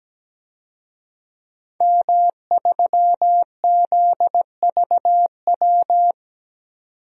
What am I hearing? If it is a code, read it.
Morse code, heard as a keyed tone, about 17 wpm, 700 Hz. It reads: M3ZVW
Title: DXCC-100 Call Signs - 5 Characters - 17wpm